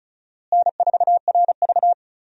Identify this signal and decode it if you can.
Morse code, heard as a keyed tone, about 36 wpm, 700 Hz. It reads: N4RV
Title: USA Call Signs - 4 Characters - Repeated 3x - 35wpm